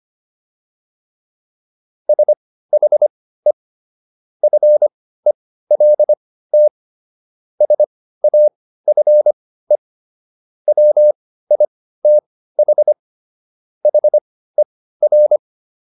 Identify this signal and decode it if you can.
Morse code, heard as a keyed tone, about 25 wpm, spaced as for 15 wpm, 600 Hz. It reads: SHE FELT SAFE WITH HER